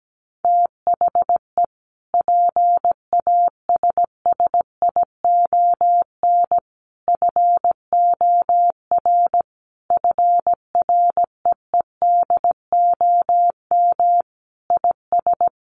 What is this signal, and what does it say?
Morse code, heard as a keyed tone, about 17 wpm, 700 Hz. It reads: THE PASSION FOR FREEDOM IS
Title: Sentences from Top 1300 Words 17wpm